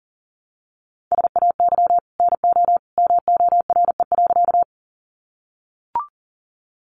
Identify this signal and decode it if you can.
Morse code, heard as a keyed tone, about 40 wpm, 700 Hz. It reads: SAY NO MORE.
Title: Sentences from Top 200 Words 40wpm